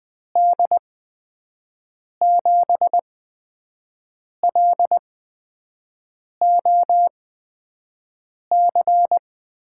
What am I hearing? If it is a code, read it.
Morse code, heard as a keyed tone, about 20 wpm, spaced as for 5 wpm, 700 Hz. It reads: D7LOC
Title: USA Call Signs - 8x Character Spacing - 20wpm